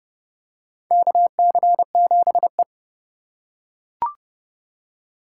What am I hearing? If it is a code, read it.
Morse code, heard as a keyed tone, about 30 wpm, 700 Hz. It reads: KC7E
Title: USA Call Signs - 4x Character Spacing - 30wpm